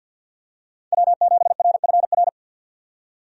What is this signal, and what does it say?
Morse code, heard as a keyed tone, about 47 wpm, 700 Hz. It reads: W7RFR
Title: USA Call Signs - 5 Characters - 50wpm